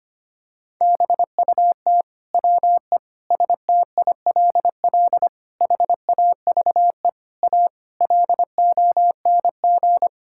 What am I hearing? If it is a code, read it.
Morse code, heard as a keyed tone, about 25 wpm, 700 Hz. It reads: BUT WE STILL HAVE A LONG
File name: Sets of 9 Words - 25wpm